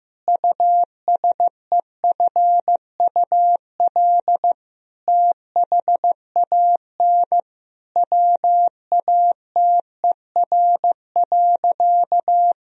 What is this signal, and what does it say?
Morse code, heard as a keyed tone, about 15 wpm, 700 Hz. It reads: USEFUL THAN WATER.
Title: Sentences from Top 1000 Words - Review Part 4 - 15wpm